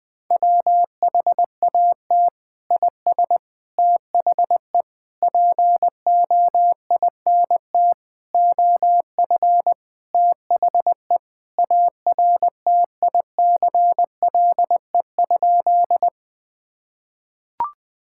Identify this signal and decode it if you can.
Morse code, heard as a keyed tone, about 20 wpm, 700 Hz. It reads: WHAT IS THE POINT OF THE ARTICLE?
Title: Sentences from Top 1000 Words - Review Part 5 - 20wpm